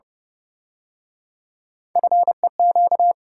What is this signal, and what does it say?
Morse code, heard as a keyed tone, about 31 wpm, 700 Hz. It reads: FEQ
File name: Speed-Racing - Three Letter-Number Combinations - 25wpm